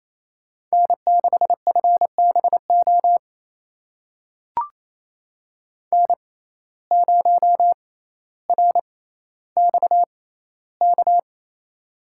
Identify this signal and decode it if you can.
Morse code, heard as a keyed tone, about 29 wpm, spaced as for 14 wpm, 700 Hz. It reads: N6FBO N0RXK
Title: USA Call Signs - 5 Characters - 6x Character Spacing - 28wpm